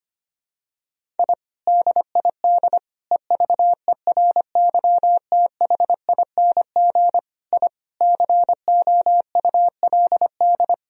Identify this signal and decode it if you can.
Morse code, heard as a keyed tone, about 25 wpm, 700 Hz. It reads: I DID EVERYTHING I COULD
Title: Sentences from Top 1000 Words - Review Part 1 - 25wpm